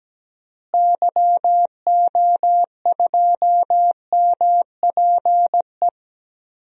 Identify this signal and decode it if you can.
Morse code, heard as a keyed tone, about 17 wpm, 700 Hz. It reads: YO2MPE
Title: DXCC-100 Call Signs - 6 Characters - Repeated 2x - 17wpm